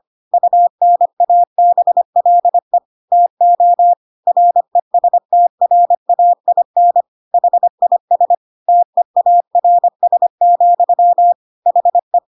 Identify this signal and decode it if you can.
Morse code, heard as a keyed tone, about 25 wpm, 700 Hz. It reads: UNABLE TO RESTRAIN HIS TEARS, HE